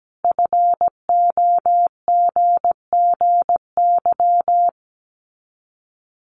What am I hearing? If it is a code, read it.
Morse code, heard as a keyed tone, about 17 wpm, 700 Hz. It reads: FOGGY
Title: Common QSO Phrases 17wpm